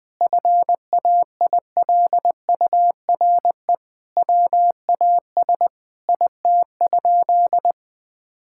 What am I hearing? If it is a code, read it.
Morse code, heard as a keyed tone, about 20 wpm, 700 Hz. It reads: FAILURE WAS IT?
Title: Sentences from Top 1000 Words - Review Part 6 - 20wpm